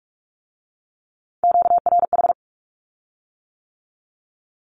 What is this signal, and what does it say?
Morse code, heard as a keyed tone, about 46 wpm, 700 Hz. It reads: QRH
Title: Q-Codes with Spoken Meaning - 45wpm